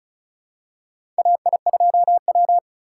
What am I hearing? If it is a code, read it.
Morse code, heard as a keyed tone, about 36 wpm, 700 Hz. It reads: AI2W
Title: USA Call Signs - 4 Characters - 2x Character Spacing - 35wpm